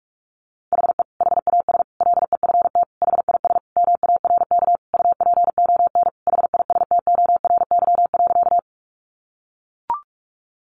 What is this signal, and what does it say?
Morse code, heard as a keyed tone, about 44 wpm, 700 Hz. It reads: HE HAS LEFT HIS MARK UPON HISTORY.
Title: Sentences from Top 1100 Words 45wpm